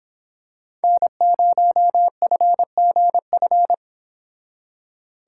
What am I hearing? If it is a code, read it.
Morse code, heard as a keyed tone, about 26 wpm, 700 Hz. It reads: N0FGF
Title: Speed Racing - USA 1x3 Call Signs 22wpm